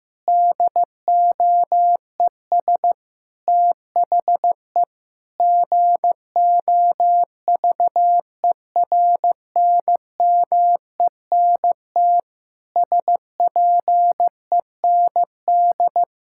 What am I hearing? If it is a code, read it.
Morse code, heard as a keyed tone, about 15 wpm, 700 Hz. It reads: DOES THE GOVERNMENT SPEND
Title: Sentences from Top 1000 Words - Review Part 5 - 15wpm